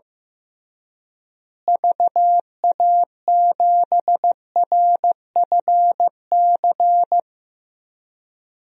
Morse code, heard as a keyed tone, about 15 wpm, 700 Hz. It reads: VA7RFC
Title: DXCC-100 Call Signs - 6 Characters - Repeated 3x - 15wpm